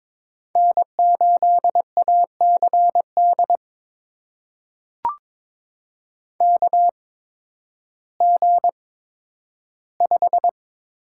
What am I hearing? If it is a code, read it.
Morse code, heard as a keyed tone, about 22 wpm, spaced as for 12 wpm, 700 Hz. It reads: N8ACD KG5
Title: USA Call Signs - 5 Characters - 8x Character Spacing - 22wpm